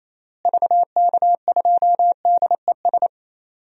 Morse code, heard as a keyed tone, about 27 wpm, 700 Hz. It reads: VK2DES
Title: DXCC-100 Call Signs - 6 Characters - Repeated 2x - 28wpm